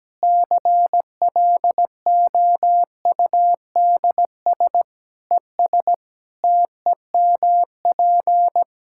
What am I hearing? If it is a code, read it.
Morse code, heard as a keyed tone, about 17 wpm, 700 Hz. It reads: CLOUDS ES TEMP
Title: Practice QSO - Exchange 2 - Info - 17wpm